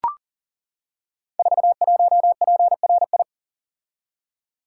Morse code, heard as a keyed tone, about 40 wpm, 690 Hz. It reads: 41PRI
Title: Code Groups - 5 Letter:Numbers - 40wpm